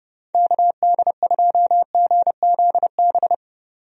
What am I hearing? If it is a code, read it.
Morse code, heard as a keyed tone, about 30 wpm, 700 Hz. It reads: KD2GZB
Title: USA Call Signs - 6 Characters - 30wpm